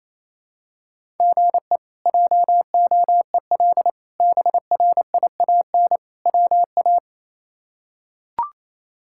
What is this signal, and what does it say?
Morse code, heard as a keyed tone, about 28 wpm, 700 Hz. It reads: GE JOEL BRIAN WA
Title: K1USN SST QSO Practice - 28wpm